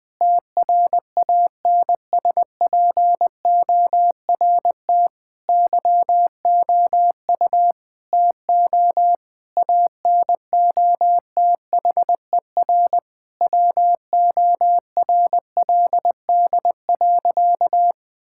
Morse code, heard as a keyed tone, about 19 wpm, 700 Hz. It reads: TRANSPORT YOU TO ANOTHER WORLD.
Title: Difficult Sentences with Hints - 20wpm